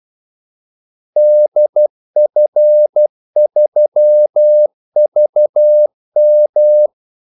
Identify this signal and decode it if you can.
Morse code, heard as a keyed tone, about 12 wpm, 600 Hz. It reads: DF3VM
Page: Kaldesignaler 12 wpm